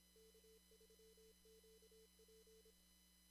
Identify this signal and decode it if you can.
Morse code, heard as a keyed tone, about 26 wpm, 440 Hz. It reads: K3QP
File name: CW / kt5j wpx cw-op-k5pi